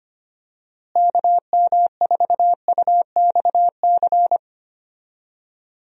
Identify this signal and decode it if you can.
Morse code, heard as a keyed tone, about 25 wpm, 700 Hz. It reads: KM4UXC
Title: USA Call Signs - 6 Characters - Repeated 2x - 25wpm